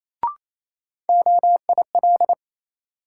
Morse code, heard as a keyed tone, about 28 wpm, 700 Hz. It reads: OIL
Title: Sentences from Top 1000 Words - Review Part 4 - New Word Drill - 28wpm